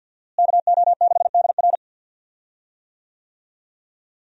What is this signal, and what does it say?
Morse code, heard as a keyed tone, about 45 wpm, 700 Hz. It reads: KO6DR
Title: USA Call Signs - 5 Characters - Repeated 2x - 50wpm